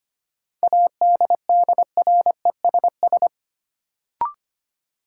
Morse code, heard as a keyed tone, about 25 wpm, 700 Hz. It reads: ADDRESS
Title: Sets of 5 Words Encore - New Word Drill - 25wpm